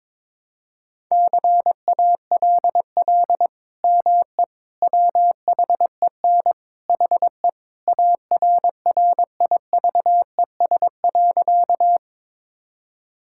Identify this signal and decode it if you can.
Morse code, heard as a keyed tone, about 22 wpm, 700 Hz. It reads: CALL ME WHEN HE ARRIVES.
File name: practice / 22wpm / Sentences with 2-Word Phrases - 22wpm